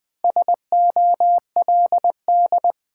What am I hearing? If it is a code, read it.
Morse code, heard as a keyed tone, about 20 wpm, 700 Hz. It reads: SOLD
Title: Sentences from Top 1400 Words - New Word Drill Alt - 20wpm